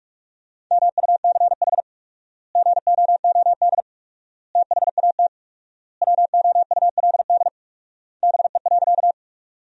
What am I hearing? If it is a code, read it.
Morse code, heard as a keyed tone, about 46 wpm, 700 Hz. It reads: MUCH GOOD THAT WOULD BE.
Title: 4x Word-Spacing - Sentences from Top 200 Words - 45wpm